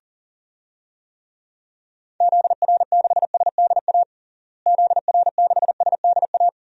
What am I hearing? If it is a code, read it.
Morse code, heard as a keyed tone, about 40 wpm, 700 Hz. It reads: ZR6SDA ZR6SDA
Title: DXCC-100 Call Signs - 6 Characters - Repeated 3x - 40wpm